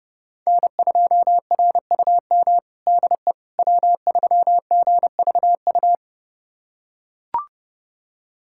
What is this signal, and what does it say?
Morse code, heard as a keyed tone, about 30 wpm, 700 Hz. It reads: N2RUM DE W3GVU
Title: USA Callsign DE Callsign - 5 Characters - 30wpm